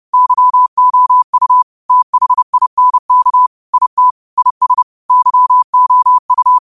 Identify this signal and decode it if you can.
Morse code, heard as a keyed tone, about 30 wpm, 1000 Hz. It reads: YOU THINK IT IS YOU
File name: qotdfiles / 30wpm